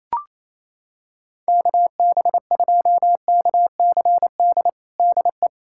Morse code, heard as a keyed tone, about 28 wpm, 700 Hz. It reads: KB2KCD DE
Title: USA Callsign DE Callsign 28wpm